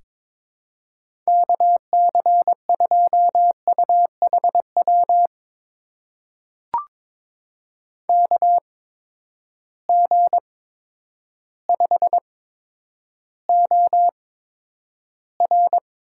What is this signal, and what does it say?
Morse code, heard as a keyed tone, about 22 wpm, spaced as for 10 wpm, 700 Hz. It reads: KC2UHW KG5OR
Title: USA Call Signs - 6 Characters - 8x Character Spacing - 22wpm